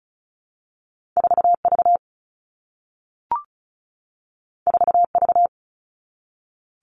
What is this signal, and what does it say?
Morse code, heard as a keyed tone, about 36 wpm, 700 Hz. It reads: 4V 4V
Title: All Cut Numbers - 2-Digit 35wpm